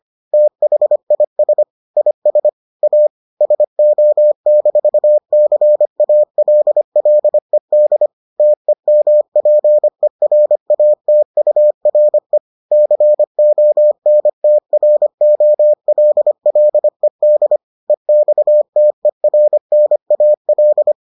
Morse code, heard as a keyed tone, about 25 wpm, 600 Hz. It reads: THIS IS A SO-CALLED TEMPERATURE CONTROLLED EXTERNAL